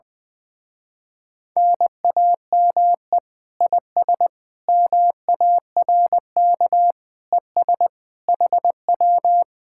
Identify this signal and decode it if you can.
Morse code, heard as a keyed tone, about 20 wpm, 700 Hz. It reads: NAME IS MARK ES HW
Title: Speed-Racing Common QSO Phrases - 20wpm